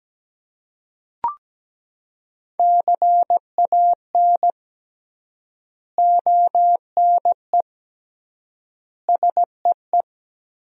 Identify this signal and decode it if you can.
Morse code, heard as a keyed tone, about 17 wpm, 700 Hz. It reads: CAN ONE SEE
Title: 3x Word-Spacing - Sentences from Top 100 Words - 17wpm